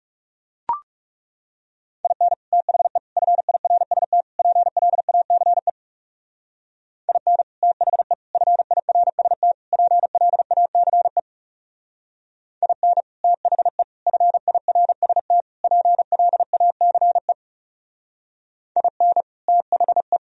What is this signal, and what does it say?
Morse code, heard as a keyed tone, about 38 wpm, 700 Hz. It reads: IN THE FIRST PLACE IN THE FIRST PLACE IN THE FIRST PLACE IN THE